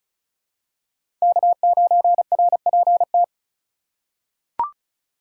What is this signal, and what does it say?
Morse code, heard as a keyed tone, about 34 wpm, 700 Hz. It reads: K9RPT
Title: USA Call Signs - 5 Characters - 4x Character Spacing - 35wpm